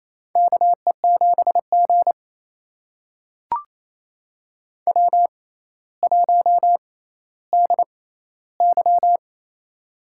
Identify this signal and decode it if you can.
Morse code, heard as a keyed tone, about 28 wpm, spaced as for 14 wpm, 700 Hz. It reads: KE7G W1DY
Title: USA Call Signs - 4 Characters - 6x Character Spacing - 28wpm